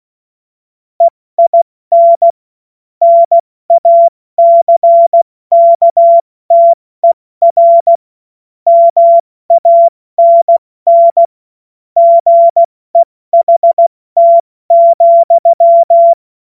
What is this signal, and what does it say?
Morse code, heard as a keyed tone, about 16 wpm, 680 Hz. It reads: EIN NACKTER MANN GEHT,
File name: Morsekurs / Witze / Witz 18. Januar 2022, 70 F80